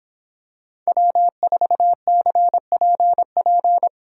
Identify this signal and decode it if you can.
Morse code, heard as a keyed tone, about 24 wpm, 700 Hz. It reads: W4CPP
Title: Speed Racing - USA 1x3 Call Signs 22wpm